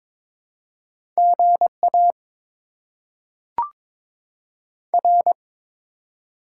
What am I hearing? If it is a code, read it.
Morse code, heard as a keyed tone, about 21 wpm, 700 Hz. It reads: GA R
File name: Top 90 QSO Elements - 4x Character Spacing - 22wpm